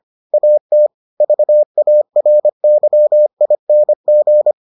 Morse code, heard as a keyed tone, about 25 wpm, 600 Hz. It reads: AT VARYING